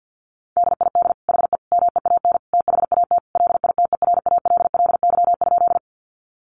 Morse code, heard as a keyed tone, about 50 wpm, 700 Hz. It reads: DID HE MEAN THAT LITERALLY?
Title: Sentences from Top 1400 Words 50wpm